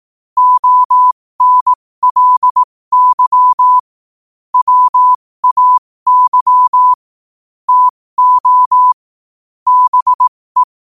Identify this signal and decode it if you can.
Morse code, heard as a keyed tone, about 18 wpm, spaced as for 15 wpm, 1000 Hz. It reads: ONLY WAY TO BE
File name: qotdfiles / 15wpm